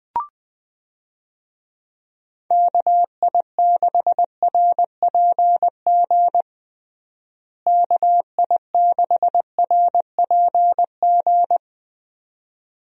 Morse code, heard as a keyed tone, about 20 wpm, 700 Hz. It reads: KI6RPG KI6RPG
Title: USA Call Signs - 6 Characters - Repeated 2x - 20wpm